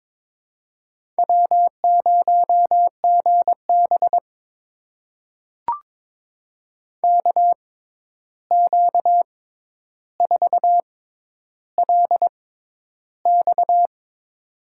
Morse code, heard as a keyed tone, about 22 wpm, spaced as for 11 wpm, 700 Hz. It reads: W0GB KQ4LX
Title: USA Call Signs - 6x Character Spacing - 22wpm